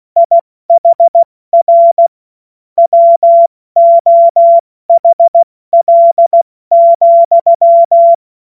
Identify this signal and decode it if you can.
Morse code, heard as a keyed tone, about 16 wpm, spaced as for 14 wpm, 680 Hz. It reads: IHR WOHL,
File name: Morsekurs / Witze / Witz 21.November 2023 70 F80